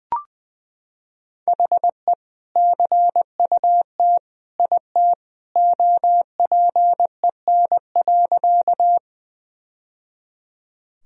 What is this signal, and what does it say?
Morse code, heard as a keyed tone, about 20 wpm, 700 Hz. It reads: HE CUT IT OPEN.
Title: Sentences from Top 1000 Words - Review Part 1 - 20wpm